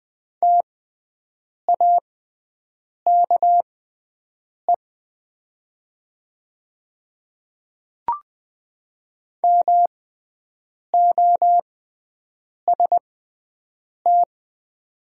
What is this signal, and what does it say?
Morse code, heard as a keyed tone, about 20 wpm, spaced as for 7 wpm, 700 Hz. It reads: TAKE MOST
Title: 6x Character Spacing - Top 300 Words - 20wpm